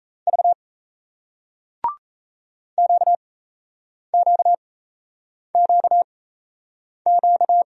Morse code, heard as a keyed tone, about 38 wpm, spaced as for 8 wpm, 700 Hz. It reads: V QQQQ